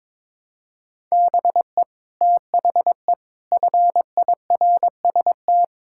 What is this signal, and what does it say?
Morse code, heard as a keyed tone, about 22 wpm, 700 Hz. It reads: BE THE FIRST
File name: practice / 22wpm / Sets of 8 Words - 22wpm